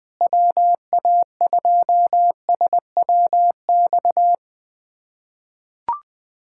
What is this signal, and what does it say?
Morse code, heard as a keyed tone, about 20 wpm, 700 Hz. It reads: WA2SWX
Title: K1USN SST QSO Practice - 20wpm